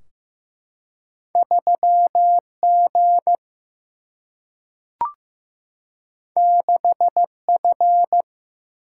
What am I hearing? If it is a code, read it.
Morse code, heard as a keyed tone, about 15 wpm, 700 Hz. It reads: 3G 6F
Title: Single-Letter Single-Number 15wpm